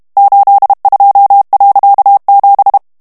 Morse code, heard as a keyed tone, about 32 wpm, 800 Hz. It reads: 82.7
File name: morse mp3 / HOLLANDAIS texte clair 1 vitesse 30 mots minute